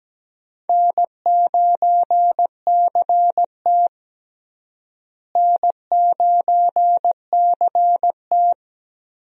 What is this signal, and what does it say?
Morse code, heard as a keyed tone, about 17 wpm, 700 Hz. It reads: N9CT N9CT
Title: USA Call Signs - 4 Characters - Repeated 3x - 17wpm